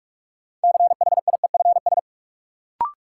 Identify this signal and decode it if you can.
Morse code, heard as a keyed tone, about 46 wpm, 700 Hz. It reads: CHIEFS
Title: Sets of 5 Words Encore - New Word Drill - 45wpm